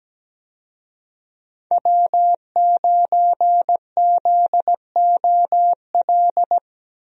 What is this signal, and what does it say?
Morse code, heard as a keyed tone, about 17 wpm, 700 Hz. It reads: W9ZOL
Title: USA Call Signs - 5 Characters - 17wpm